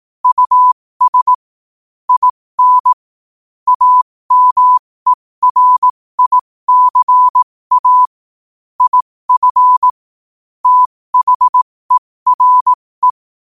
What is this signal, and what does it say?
Morse code, heard as a keyed tone, about 18 wpm, spaced as for 15 wpm, 1000 Hz. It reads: US IN AMERICA IF THERE
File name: qotdfiles / 15wpm